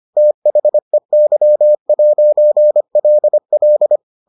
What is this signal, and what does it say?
Morse code, heard as a keyed tone, about 25 wpm, 600 Hz. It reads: THEY'LL